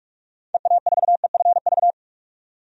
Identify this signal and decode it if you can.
Morse code, heard as a keyed tone, about 46 wpm, 700 Hz. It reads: EA4EFV
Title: DXCC-100 Call Signs - 6 Characters - 45wpm